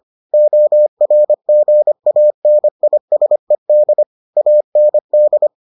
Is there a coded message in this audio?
Yes: Morse code from 0.3 to 5.5 s, heard as a keyed tone, about 25 wpm, 600 Hz. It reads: ORGANISED AND